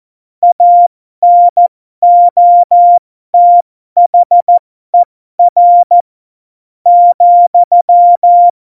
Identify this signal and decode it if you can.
Morse code, heard as a keyed tone, about 14 wpm, spaced as for 11 wpm, 700 Hz. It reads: ANOTHER,
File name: Morsekurs / Witze